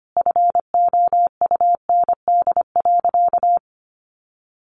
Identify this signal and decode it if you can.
Morse code, heard as a keyed tone, about 25 wpm, 700 Hz. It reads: FOUND.
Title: Sentences from Top 300 Words 25wpm